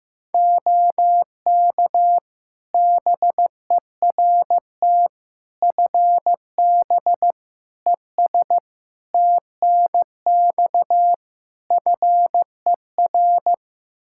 Morse code, heard as a keyed tone, about 15 wpm, 700 Hz. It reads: OK BERT FB ES TNX FER
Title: Practice QSO - Exchange 2 - Info - 15wpm